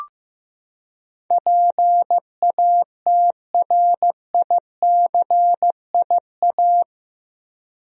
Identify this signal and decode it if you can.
Morse code, heard as a keyed tone, about 15 wpm, 700 Hz. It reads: PATRICIA
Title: USA Common Names - 15wpm